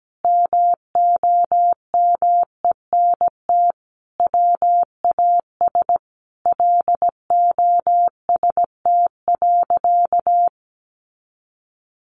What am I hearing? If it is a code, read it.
Morse code, heard as a keyed tone, about 17 wpm, 700 Hz. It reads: MOMENT WAS LOST.
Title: Sentences from Top 1400 Words 17wpm